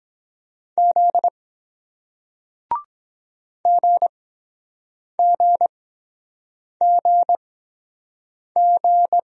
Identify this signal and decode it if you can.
Morse code, heard as a keyed tone, about 23 wpm, spaced as for 7 wpm, 700 Hz. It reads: Z GGGG